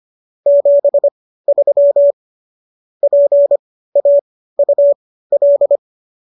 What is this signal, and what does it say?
Morse code, heard as a keyed tone, about 25 wpm, spaced as for 15 wpm, 570 Hz. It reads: 73 PAUL